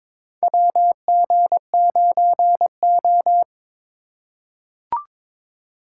Morse code, heard as a keyed tone, about 22 wpm, 700 Hz. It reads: WG9O
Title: USA Call Signs - 4 Characters - 3x Character Spacing - 22wpm